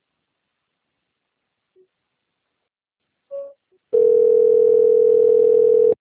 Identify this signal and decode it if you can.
Ringback tone (North America), 440 + 480 Hz.
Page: Recife - PE